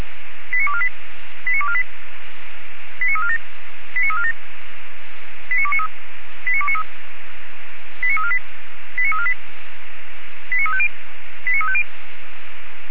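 Five-tone selective call (ZVEI), heard as quick runs of five tones, tones of about 70 ms. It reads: 89248 89248 89347 89347 89293 89293 89348 89348 89350 89350
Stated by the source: ZVEI five-tone selective call received via SDR.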